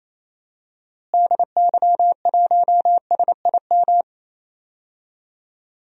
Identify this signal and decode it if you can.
Morse code, heard as a keyed tone, about 28 wpm, 700 Hz. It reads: DY1SIM
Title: DXCC-100 Call Signs - 6 Characters - Repeated 3x - 28wpm